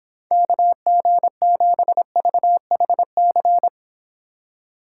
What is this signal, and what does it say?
Morse code, heard as a keyed tone, about 26 wpm, 700 Hz. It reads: KG7VHC